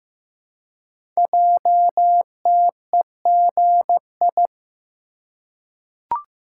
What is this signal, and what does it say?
Morse code, heard as a keyed tone, about 15 wpm, 700 Hz. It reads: JTEGI
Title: Code Groups - 5 Letters - 15wpm